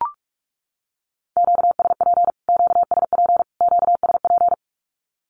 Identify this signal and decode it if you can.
Morse code, heard as a keyed tone, about 44 wpm, 700 Hz. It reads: QSP QSP QSP
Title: Q-Codes Repeated 3x 45wpm